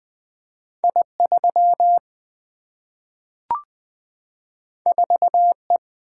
Morse code, heard as a keyed tone, about 20 wpm, 700 Hz. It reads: I 3 4 E
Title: Single-Letter Single-Number 20wpm